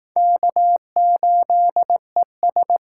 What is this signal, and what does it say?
Morse code, heard as a keyed tone, about 19 wpm, 700 Hz. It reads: K8ES